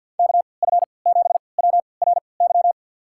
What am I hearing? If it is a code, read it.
Morse code, heard as a keyed tone, about 53 wpm, spaced as for 29 wpm, 700 Hz. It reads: KF7WRY
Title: USA Call Signs - 3x Character Spacing - 50wpm